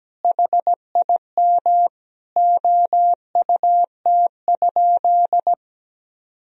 Morse code, heard as a keyed tone, about 17 wpm, 700 Hz. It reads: HIM OUT?
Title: Sentences from Top 1000 Words - Review Part 3 - 17wpm